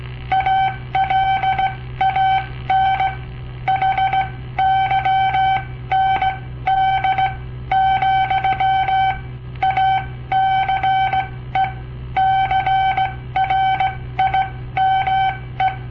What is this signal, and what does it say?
Morse code, heard as a keyed tone, about 18 wpm, spaced as for 14 wpm, 770 Hz. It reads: ALAN HYND, ACE CRIME
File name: papertapes / local / Code Practice-01